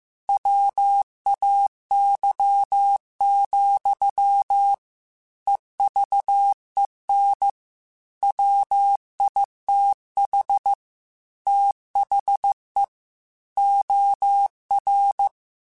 Morse code, heard as a keyed tone, about 15 wpm, 800 Hz. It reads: WAY, EVEN WITH THE OR
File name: Technical Subjects / Technical Subjects-mp3s / OR Gate @13